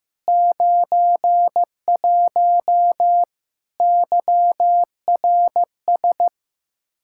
Morse code, heard as a keyed tone, about 16 wpm, 700 Hz. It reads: 91 YRS
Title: Practice QSO - Exchange 3 - More Info - 15wpm